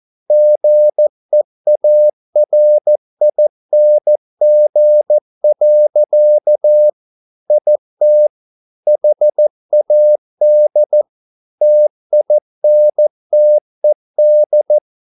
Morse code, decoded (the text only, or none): GEARING. IT HAD TINTED